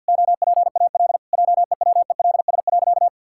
Morse code, heard as a keyed tone, about 50 wpm, 700 Hz. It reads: OPAL JEWELS.
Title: Pangram Challenge v2 - 50wpm